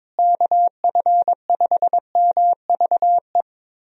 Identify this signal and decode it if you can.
Morse code, heard as a keyed tone, about 22 wpm, 700 Hz. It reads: KF5MVE
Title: USA Call Signs - 6 Characters - Repeated 3x - 22wpm